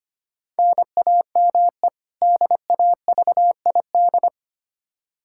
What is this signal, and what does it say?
Morse code, heard as a keyed tone, about 25 wpm, 700 Hz. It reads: NAME DAVID
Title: K1USN SST QSO Practice 25wpm